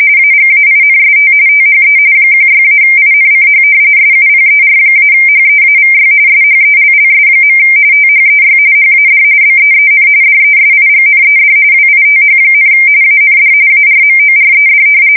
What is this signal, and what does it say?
RTTY, about 45.45 baud. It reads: AGAINST THE 16-INCH SUPER-HEAVY SHELLS. THEY HAD BEEN DESIGNED WITH, AND ARMORED
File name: fsk170 / battleships